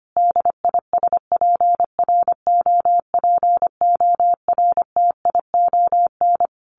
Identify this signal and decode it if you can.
Morse code, heard as a keyed tone, about 25 wpm, 700 Hz. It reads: DISPROPORTION
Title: A Sound Of Thunder 25wpm